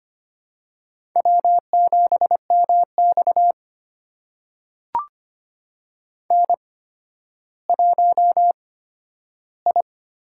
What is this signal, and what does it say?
Morse code, heard as a keyed tone, about 25 wpm, spaced as for 12 wpm, 700 Hz. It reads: W7MX N1I
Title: USA Call Signs - 4 Characters - 8x Character Spacing - 25wpm